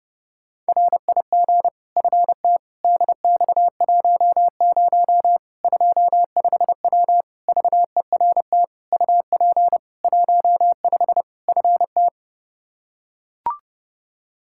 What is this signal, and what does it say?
Morse code, heard as a keyed tone, about 32 wpm, 700 Hz. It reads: RIG FT DX10 25W VERT UP 15 FT
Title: K1USN SST QSO Practice 30wpm